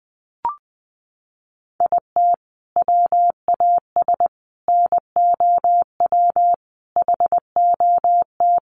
Morse code, heard as a keyed tone, about 20 wpm, 700 Hz. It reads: IT WAS NOW HOT
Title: Sentences from Top 1200 Words 20wpm